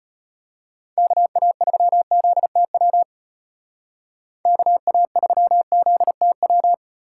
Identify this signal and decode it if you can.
Morse code, heard as a keyed tone, about 30 wpm, 700 Hz. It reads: KA3ZTW KA3ZTW
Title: Speed Racing - USA 2x3 Call Signs 25wpm